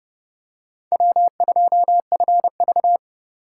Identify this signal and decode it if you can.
Morse code, heard as a keyed tone, about 28 wpm, 700 Hz. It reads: W2FV